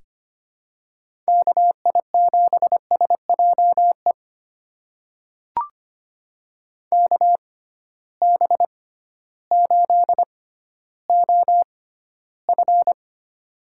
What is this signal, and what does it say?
Morse code, heard as a keyed tone, about 25 wpm, spaced as for 14 wpm, 700 Hz. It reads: KI7SJE KB8OF
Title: USA Call Signs - 6x Character Spacing - 25wpm